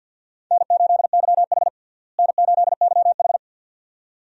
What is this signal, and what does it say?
Morse code, heard as a keyed tone, about 52 wpm, 700 Hz. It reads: N8YH N8YH
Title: USA Call Signs - 4 Characters - Repeated 2x - 50wpm